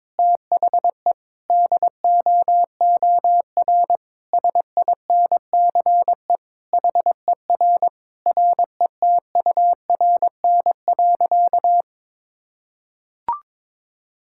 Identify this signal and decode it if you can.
Morse code, heard as a keyed tone, about 22 wpm, 700 Hz. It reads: THE DOOR SINCE HER RETURN.
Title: Sentences from Top 1000 Words - Review Part 6 - 22wpm